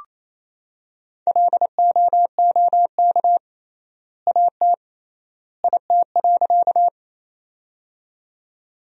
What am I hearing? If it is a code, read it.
Morse code, heard as a keyed tone, about 27 wpm, 700 Hz. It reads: LOOK AT IT.
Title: 3x Word-Spacing - Sentences from Top 100 Words - 28wpm